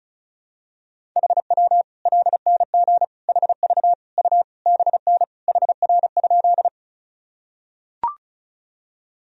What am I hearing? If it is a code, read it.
Morse code, heard as a keyed tone, about 36 wpm, 700 Hz. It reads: HW LNG HV U BN HR?
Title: Sentences from Top 200 Words and Abbreviations 35wpm